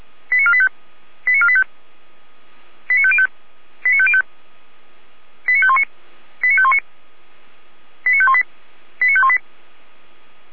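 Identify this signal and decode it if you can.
Five-tone selective call (ZVEI), heard as quick runs of five tones, tones of about 70 ms. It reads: 89485 89485 89505 89505 89519 89519 89518 89518
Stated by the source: ZVEI five-tone call 89361 received on SDR, from audio.f85360.feuerwehr-orlamuende.org (fire brigade Orlamünde).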